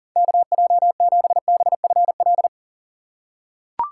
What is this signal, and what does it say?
Morse code, heard as a keyed tone, about 39 wpm, 700 Hz. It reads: KJ7BFL
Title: USA Call Signs - 6 Characters - 4x Character Spacing - 40wpm